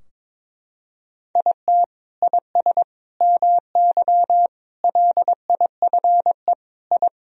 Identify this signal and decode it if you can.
Morse code, heard as a keyed tone, about 22 wpm, 700 Hz. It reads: IT IS MY LIFE I
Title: Sentences from Top 1000 Words - Review Part 2 - 22wpm